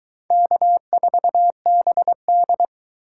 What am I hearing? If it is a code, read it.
Morse code, heard as a keyed tone, about 23 wpm, 700 Hz. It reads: K4BD